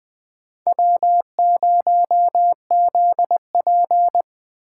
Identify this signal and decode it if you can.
Morse code, heard as a keyed tone, about 20 wpm, 700 Hz. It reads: W0ZP
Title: K1USN SST QSO Practice - 20wpm